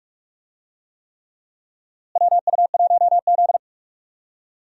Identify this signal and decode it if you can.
Morse code, heard as a keyed tone, about 44 wpm, 700 Hz. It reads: WU1Z
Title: USA Call Signs - 4 Characters - Repeated 3x - 45wpm